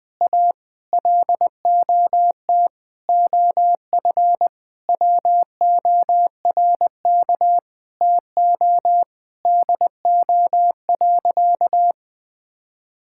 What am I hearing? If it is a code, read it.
Morse code, heard as a keyed tone, about 20 wpm, 700 Hz. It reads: A LOT OF WORK TO DO.
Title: Sentences with 9-Word Phrases - Spoken Beginning - 20wpm